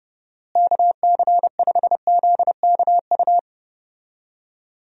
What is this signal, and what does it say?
Morse code, heard as a keyed tone, about 31 wpm, 700 Hz. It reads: KC5ZKU
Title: USA Call Signs - 6 Characters - Repeated 3x - 30wpm